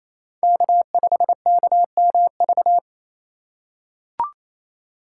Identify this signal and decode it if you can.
Morse code, heard as a keyed tone, about 28 wpm, 700 Hz. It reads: K5KMV
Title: USA Call Signs - 3x Character Spacing - 28wpm